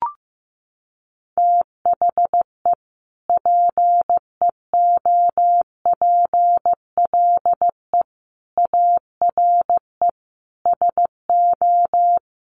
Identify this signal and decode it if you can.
Morse code, heard as a keyed tone, about 15 wpm, 700 Hz. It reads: THE PEOPLE ARE SO
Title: Sentences from Top 1000 Words 15wpm